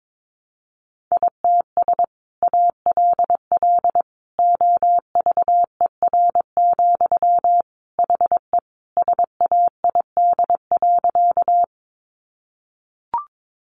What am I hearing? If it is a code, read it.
Morse code, heard as a keyed tone, about 22 wpm, 700 Hz. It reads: ITS ALL OVER, HE SAID.
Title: Sentences from Top 600 Words 22wpm